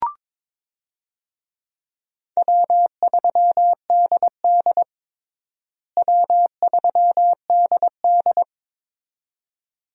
Morse code, heard as a keyed tone, about 22 wpm, 700 Hz. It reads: W3DD W3DD
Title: USA Call Signs - 4 Characters - Repeated 2x - 22wpm